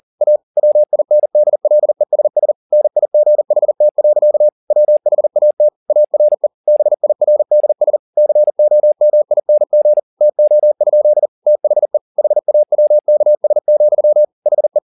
Morse code, heard as a keyed tone, about 40 wpm, 600 Hz. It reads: A WINDLESS NIGHT. WHAT ARE BIRDS COMING TO? THE HAWKS, HE